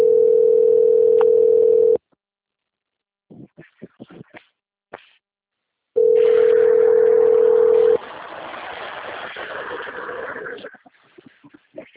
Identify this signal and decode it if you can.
Ringback tone (North America), 440 + 480 Hz.
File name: out- - - - - .42649